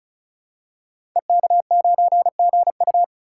Morse code, heard as a keyed tone, about 36 wpm, 700 Hz. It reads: EK9GU
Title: Code Groups - 5 Letter:Numbers - 35wpm